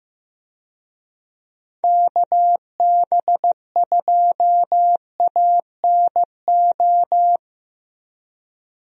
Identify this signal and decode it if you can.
Morse code, heard as a keyed tone, about 16 wpm, 700 Hz. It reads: KB2ANO
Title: USA Call Signs - 6 Characters - Repeated 3x - 15wpm